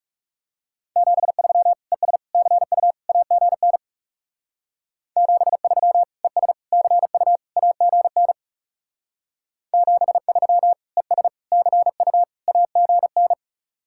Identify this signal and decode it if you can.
Morse code, heard as a keyed tone, about 36 wpm, 700 Hz. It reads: 73 ES CU AGN 73 ES CU AGN 73 ES CU AGN
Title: Speed-Racing Common QSO Phrases - 30wpm